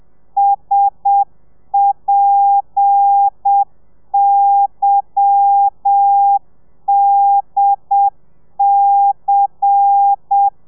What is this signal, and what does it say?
Morse code, heard as a keyed tone, about 7 wpm, 800 Hz. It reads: SPYDC
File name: telegrafia / i2tfj / Lez10 30